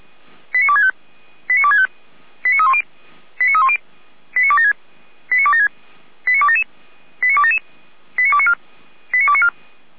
Five-tone selective call (ZVEI), heard as quick runs of five tones, tones of about 80 ms. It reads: 89265 89265 89310 89310 89276 89276 89277 89277 89294 89294
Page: /455/SDR/ZVEI/